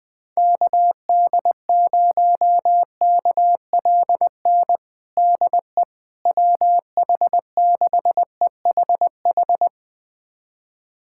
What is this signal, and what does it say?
Morse code, heard as a keyed tone, about 20 wpm, 700 Hz. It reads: KD0KLN DE WH6EHH
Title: USA Callsign DE Callsign - 6 Characters - 20wpm